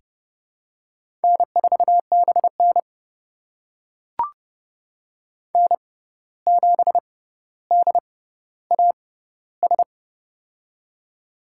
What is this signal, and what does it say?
Morse code, heard as a keyed tone, about 30 wpm, spaced as for 14 wpm, 700 Hz. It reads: N4BN N7DAS
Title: USA Call Signs - 6x Character Spacing - 30wpm